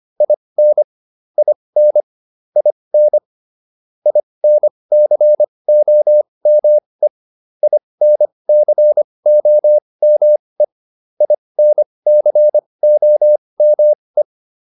Morse code, heard as a keyed tone, about 25 wpm, spaced as for 20 wpm, 600 Hz. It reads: IN IN IN INCOME INCOME INCOME